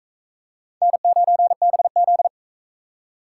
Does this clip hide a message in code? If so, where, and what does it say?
Yes: Morse code from 0.8 to 2.3 s, heard as a keyed tone, about 36 wpm, 700 Hz. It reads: N9BZ